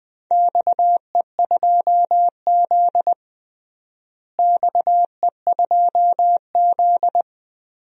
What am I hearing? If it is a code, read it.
Morse code, heard as a keyed tone, about 20 wpm, 700 Hz. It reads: XE2Z XE2Z
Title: DXCC-100 Call Signs - 4 Characters - Repeated 2x - 20wpm